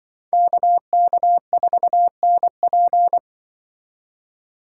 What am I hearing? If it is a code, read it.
Morse code, heard as a keyed tone, about 24 wpm, 700 Hz. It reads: KK4NP
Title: Speed Racing - USA Call Signs 20wpm